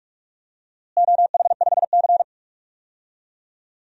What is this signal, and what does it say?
Morse code, heard as a keyed tone, about 41 wpm, 700 Hz. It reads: OH5C
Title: DXCC-100 Call Signs - 4 Characters - Repeated 3x - 45wpm